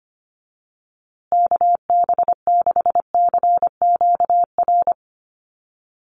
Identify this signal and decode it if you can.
Morse code, heard as a keyed tone, about 25 wpm, 700 Hz. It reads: KB6CQR
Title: USA Call Signs Repeated 3x 25wpm